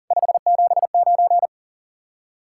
Morse code, heard as a keyed tone, about 50 wpm, 700 Hz. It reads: 579